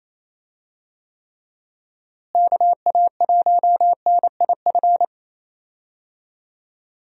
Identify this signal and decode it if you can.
Morse code, heard as a keyed tone, about 28 wpm, 700 Hz. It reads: KA1NIF
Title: USA Call Signs - 6 Characters - Repeated 3x - 28wpm